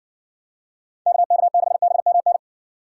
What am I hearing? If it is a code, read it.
Morse code, heard as a keyed tone, about 60 wpm, 700 Hz. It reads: KK6BMN